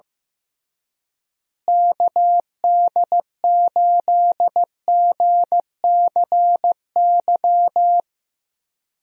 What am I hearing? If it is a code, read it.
Morse code, heard as a keyed tone, about 15 wpm, 700 Hz. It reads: KD8GCY
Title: USA Call Signs - 6 Characters - Repeated 3x - 15wpm